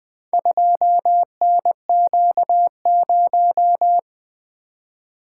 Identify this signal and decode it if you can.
Morse code, heard as a keyed tone, about 20 wpm, 700 Hz. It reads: 2NQ0
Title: Four Letter-Number Combinations - 20wpm